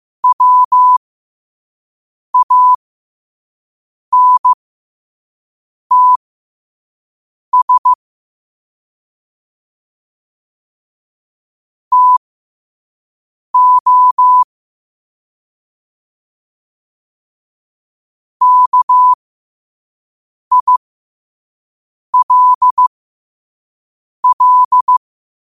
Morse code, heard as a keyed tone, about 15 wpm, spaced as for 5 wpm, 1000 Hz. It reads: WANTS TO KILL
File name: qotdfiles / 5wpm